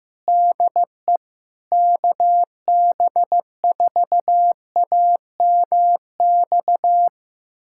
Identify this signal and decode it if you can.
Morse code, heard as a keyed tone, about 16 wpm, 700 Hz. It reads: DE KB4AMX
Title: USA Callsign DE Callsign - 6 Characters - 15wpm